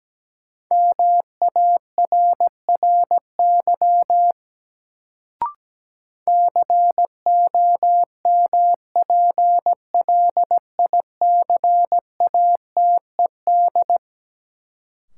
Morse code, heard as a keyed tone, about 17 wpm, 700 Hz. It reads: MARRY COMPLICATED
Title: Sentences from Top 1200 Words - New Word Drill - 17wpm